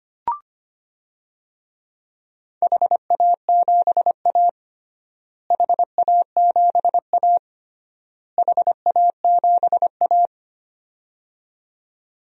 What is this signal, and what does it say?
Morse code, heard as a keyed tone, about 25 wpm, 700 Hz. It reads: HA7A HA7A HA7A
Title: DXCC-100 Call Signs - 4 Characters - Repeated 3x - 25wpm